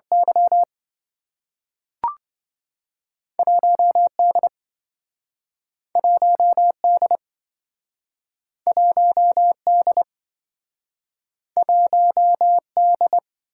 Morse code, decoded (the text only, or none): Y 1D 1D 1D 1D